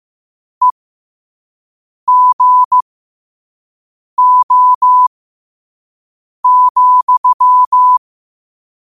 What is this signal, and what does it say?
Morse code, heard as a keyed tone, about 15 wpm, spaced as for 5 wpm, 1000 Hz. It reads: EGO,
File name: qotdfiles / 5wpm